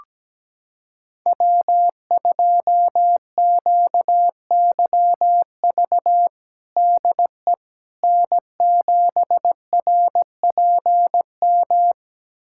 Morse code, heard as a keyed tone, about 17 wpm, 700 Hz. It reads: W2QYV DE N7RPM
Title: USA Callsign DE Callsign - 5 Characters - 17wpm